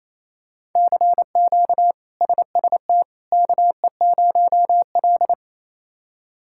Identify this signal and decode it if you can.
Morse code, heard as a keyed tone, about 28 wpm, 700 Hz. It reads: CQ SST KE0L
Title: K1USN SST QSO Practice - 28wpm